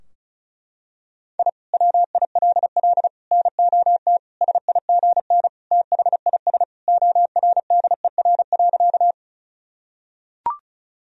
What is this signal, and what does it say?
Morse code, heard as a keyed tone, about 36 wpm, 700 Hz. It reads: I WILL NOT SIGN THIS ORDER.
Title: Sentences from Top 1000 Words - Review Part 2 - 35wpm